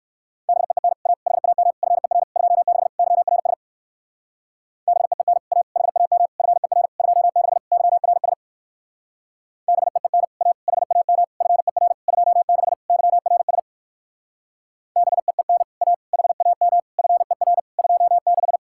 Morse code, heard as a keyed tone, about 59 wpm, 700 Hz. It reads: BEEN A HAM FER 26 YRS BEEN A HAM FER 26 YRS BEEN A HAM FER 26 YRS BEEN A HAM FER 26